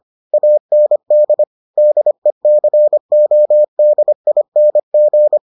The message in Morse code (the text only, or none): AND DECODING